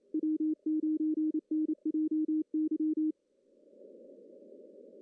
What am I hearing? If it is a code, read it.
Morse code, heard as a keyed tone, about 28 wpm, 310 Hz. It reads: W9NJY
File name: CW / cqww-cw-k5tr